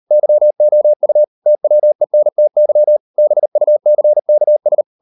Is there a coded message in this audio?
Yes: Morse code from 0.1 to 4.8 s, heard as a keyed tone, about 38 wpm, 600 Hz. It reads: YOU TWENTY BUCKS